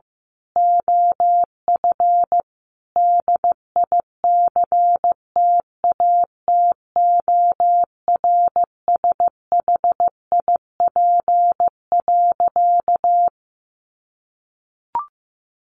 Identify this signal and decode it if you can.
Morse code, heard as a keyed tone, about 15 wpm, 700 Hz. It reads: OF DICTATORSHIP.
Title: A Sound Of Thunder 15wpm